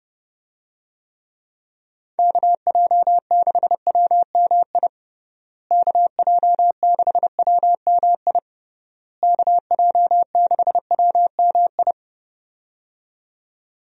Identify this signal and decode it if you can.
Morse code, heard as a keyed tone, about 30 wpm, 700 Hz. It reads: KJ6WMI KJ6WMI KJ6WMI
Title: USA Call Signs - 6 Characters - Repeated 3x - 30wpm